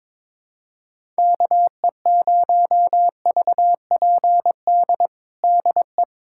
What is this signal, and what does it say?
Morse code, heard as a keyed tone, about 22 wpm, 700 Hz. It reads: KE0VPD DE
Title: USA Callsign DE Callsign - 6 Characters - 22wpm